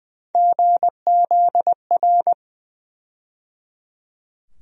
Morse code, heard as a keyed tone, about 20 wpm, 700 Hz. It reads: GZR
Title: USA Call Sign Postfixes - 3 Letters 20wpm